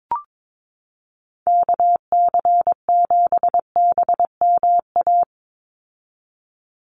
Morse code, heard as a keyed tone, about 22 wpm, 700 Hz. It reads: KC7BMA
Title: USA Call Signs 22wpm